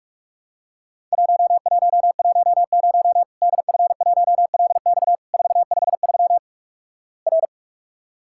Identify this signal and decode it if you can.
Morse code, heard as a keyed tone, about 44 wpm, 700 Hz. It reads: 1110 DF1LX 453 R
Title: Worked All Europe DX Contest - QTC Exchange - 45wpm